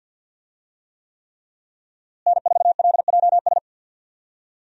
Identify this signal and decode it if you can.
Morse code, heard as a keyed tone, about 47 wpm, 700 Hz. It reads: N4LJS
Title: USA Call Signs - 5 Characters - Repeated 3x - 50wpm